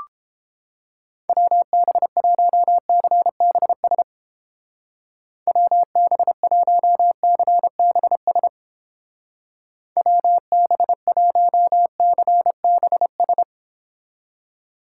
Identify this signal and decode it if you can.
Morse code, heard as a keyed tone, about 30 wpm, 700 Hz. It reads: WB1CBS WB1CBS WB1CBS